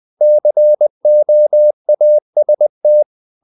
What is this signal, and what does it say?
Morse code, heard as a keyed tone, about 20 wpm, 600 Hz. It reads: COAST
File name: the old man and the sea / the old man and the sea part 1 20wpm